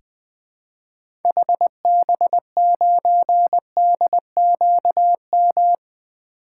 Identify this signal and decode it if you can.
Morse code, heard as a keyed tone, about 20 wpm, 700 Hz. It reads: HB9DQM
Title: 2023 SOTA Activator Call Signs - 20wpm